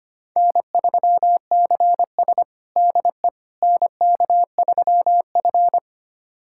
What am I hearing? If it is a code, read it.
Morse code, heard as a keyed tone, about 25 wpm, 700 Hz. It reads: N3CS DE NK3F
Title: USA Callsign DE Callsign - 4 Characters - 25wpm